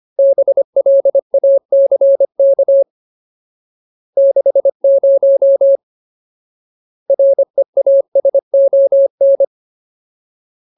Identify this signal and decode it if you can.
Morse code, heard as a keyed tone, about 25 wpm, 550 Hz. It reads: BLACK 60 REASON
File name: Ultimate Word List / 25WPM Ultimate